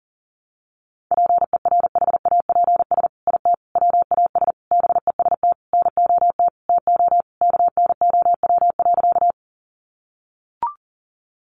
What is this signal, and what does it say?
Morse code, heard as a keyed tone, about 40 wpm, 700 Hz. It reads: PERHAPS IT WAS BEST NOT TO KNOW.
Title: Sentences from Top 600 Words 40wpm